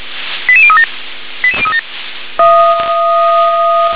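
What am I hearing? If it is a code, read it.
Five-tone selective call (ZVEI), heard as quick runs of five tones, tones of about 70 ms. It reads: 88037 88037
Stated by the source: Five-tone ZVEI selective call received on an SDR.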